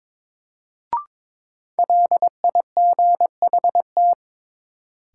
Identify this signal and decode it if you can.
Morse code, heard as a keyed tone, about 22 wpm, 700 Hz. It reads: LIGHT
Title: Sentences from Top 200 Words - New Word Drill - 22wpm